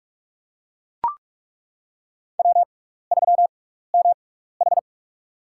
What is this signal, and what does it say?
Morse code, heard as a keyed tone, about 42 wpm, spaced as for 15 wpm, 700 Hz. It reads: W3MH
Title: USA Call Signs - 4 Characters - 6x Character Spacing - 45wpm